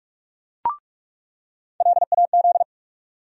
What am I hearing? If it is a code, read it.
Morse code, heard as a keyed tone, about 46 wpm, 700 Hz. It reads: LAZ
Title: USA Call Sign Postfixes - 3 Letters 45wpm